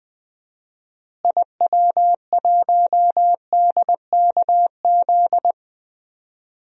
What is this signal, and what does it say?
Morse code, heard as a keyed tone, about 20 wpm, 700 Hz. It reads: IW1DKZ
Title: DXCC-100 Call Signs - 6 Characters - Repeated 3x - 20wpm